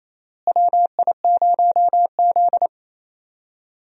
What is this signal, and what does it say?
Morse code, heard as a keyed tone, about 28 wpm, 700 Hz. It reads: WI0Z
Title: USA Call Signs - 4 Characters - Repeated 3x - 28wpm